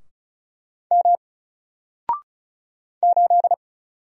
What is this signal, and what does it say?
Morse code, heard as a keyed tone, about 36 wpm, 700 Hz. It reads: M 8
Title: Single Character (Letter:Number:Punctuation:Prosign) Alt 35wpm